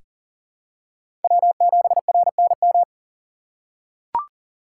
Morse code, heard as a keyed tone, about 39 wpm, 700 Hz. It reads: W7RNM
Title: USA Call Signs - 3x Character Spacing - 40wpm